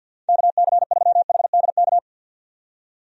Morse code, heard as a keyed tone, about 48 wpm, 700 Hz. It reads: KC3HDK
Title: USA Call Signs - 6 Characters - 50wpm